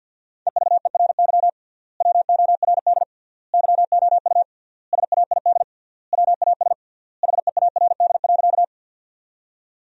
Morse code, heard as a keyed tone, about 50 wpm, 700 Hz. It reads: EVERY WORD YOU SAID WAS HEARD.
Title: 3x Word-Spacing - Sentences from Top 400 Words - 50wpm